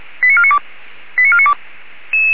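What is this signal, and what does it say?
Five-tone selective call (ZVEI), heard as quick runs of five tones, tones of about 70 ms. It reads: 89492 89492
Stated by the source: ZVEI five-tone call 89135 received on an SDR, from the feuerwehr-orlamuende.org (Orlamünde fire brigade) audio server.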